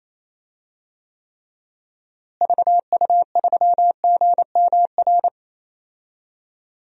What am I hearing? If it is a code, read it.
Morse code, heard as a keyed tone, about 28 wpm, 700 Hz. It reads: VU3GMR
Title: DXCC-100 Call Signs - 6 Characters - Repeated 2x - 28wpm